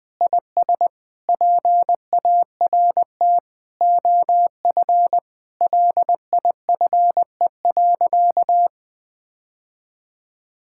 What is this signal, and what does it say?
Morse code, heard as a keyed tone, about 20 wpm, 700 Hz. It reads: IS PART OF LIFE.
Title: Sentences with 3-Word Phrases - 20wpm